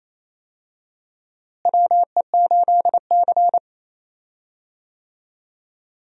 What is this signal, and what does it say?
Morse code, heard as a keyed tone, about 28 wpm, 700 Hz. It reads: WE8C
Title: USA Call Signs - 4 Characters - Repeated 2x - 28wpm